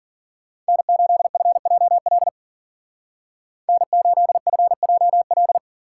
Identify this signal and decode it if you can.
Morse code, heard as a keyed tone, about 53 wpm, 700 Hz. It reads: N8FJL N8FJL